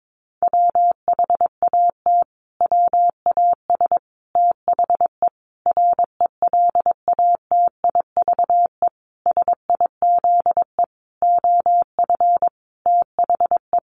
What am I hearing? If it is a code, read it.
Morse code, heard as a keyed tone, about 22 wpm, 700 Hz. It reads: WHAT WAS THE RELATIVE SIZE OF THE
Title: Sentences from Top 1400 Words 22wpm